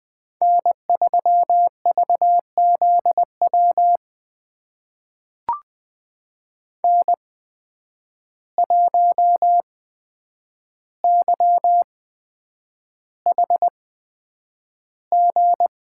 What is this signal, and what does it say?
Morse code, heard as a keyed tone, about 20 wpm, spaced as for 9 wpm, 700 Hz. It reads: N3VZW N1YHG
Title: USA Call Signs - 5 Characters - 8x Character Spacing - 20wpm